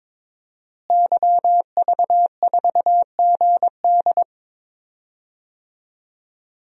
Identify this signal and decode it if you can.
Morse code, heard as a keyed tone, about 22 wpm, 700 Hz. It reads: YV4GD
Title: DXCC-100 Call Signs - 5 Characters - Repeated 2x - 22wpm